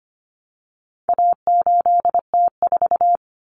Morse code, heard as a keyed tone, about 25 wpm, 700 Hz. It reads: A8T4
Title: Common Cut Numbers - 4-Digit 25wpm